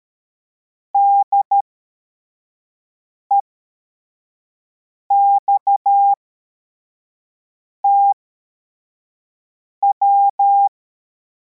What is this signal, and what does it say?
Morse code, heard as a keyed tone, about 13 wpm, spaced as for 4 wpm, 800 Hz. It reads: DEXTW
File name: formation / cw / Text61